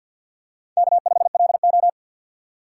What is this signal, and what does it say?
Morse code, heard as a keyed tone, about 47 wpm, 690 Hz. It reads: K5LO
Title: USA Call Signs - 2x Character Spacing - 50wpm